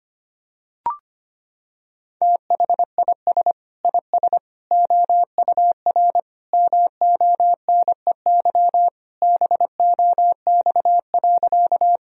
Morse code, heard as a keyed tone, about 25 wpm, 700 Hz. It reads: THIS IS OUR MONEY BOX.
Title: Sentences from Top 1000 Words - Review Part 2 - 25wpm